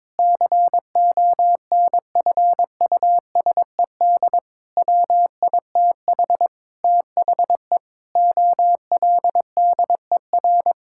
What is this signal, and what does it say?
Morse code, heard as a keyed tone, about 21 wpm, 700 Hz. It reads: CONFUSED WITH THE OLDER
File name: Sentences with 7-Word Phrases - 22wpm